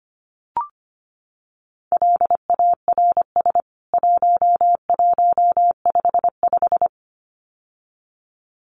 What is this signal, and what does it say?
Morse code, heard as a keyed tone, about 25 wpm, 700 Hz. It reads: LARS 1155
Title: CWT Exchange 25wpm